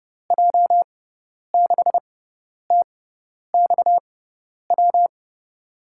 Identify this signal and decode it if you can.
Morse code, heard as a keyed tone, about 30 wpm, spaced as for 10 wpm, 700 Hz. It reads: J6TXW
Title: USA Call Signs - 6 Characters - 6x Character Spacing - 30wpm